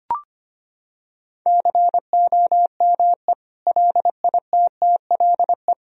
Morse code, heard as a keyed tone, about 25 wpm, 700 Hz. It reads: COME LITTLE
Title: Sentences from Top 1000 Words - Review Part 2 - 25wpm